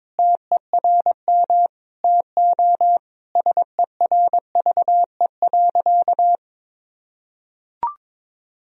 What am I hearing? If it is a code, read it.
Morse code, heard as a keyed tone, about 22 wpm, 700 Hz. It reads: TERM TO SERVE.
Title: Sentences from Top 1000 Words - Review Part 4 - 22wpm